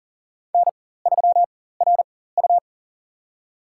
Morse code, heard as a keyed tone, about 38 wpm, spaced as for 18 wpm, 700 Hz. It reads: N3RU
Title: USA Call Signs - 4 Characters - 4x Character Spacing - 40wpm